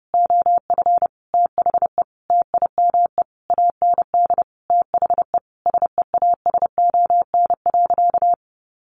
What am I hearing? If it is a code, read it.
Morse code, heard as a keyed tone, about 30 wpm, 700 Hz. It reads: OF THE TIME AND THE SEASON.
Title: Sentences from Top 1500 Words 30wpm